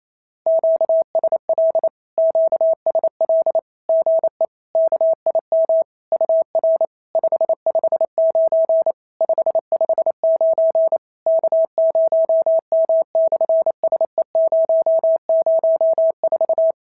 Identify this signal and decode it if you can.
Morse code, heard as a keyed tone, about 28 wpm, 650 Hz. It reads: QSL QSL GE KIM UR 559 559 K0M/SE004
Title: SOTA QSO Practice 28wpm